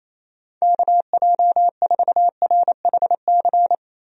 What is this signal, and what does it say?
Morse code, heard as a keyed tone, about 28 wpm, 700 Hz. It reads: KJ4RHC
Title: USA Call Signs - 6 Characters - Repeated 2x - 28wpm